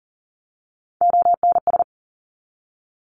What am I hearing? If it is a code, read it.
Morse code, heard as a keyed tone, about 40 wpm, 700 Hz. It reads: ONS
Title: Top Three Letter Combinations in Words 40wpm